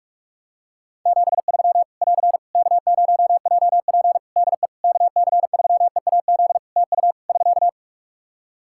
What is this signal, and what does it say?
Morse code, heard as a keyed tone, about 46 wpm, 700 Hz. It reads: 73 <AR> K0JP DE KC3EAZ TU <SK>
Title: Practice QSO - Exchange 4 - Ending - 45wpm